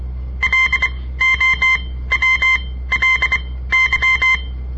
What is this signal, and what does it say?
Morse code, heard as a keyed tone, about 24 wpm, spaced as for 16 wpm, 2100 Hz. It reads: LOWLY
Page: On the air Morse code practice from the Southern Montana Amatuer Radio Association; all code, no read back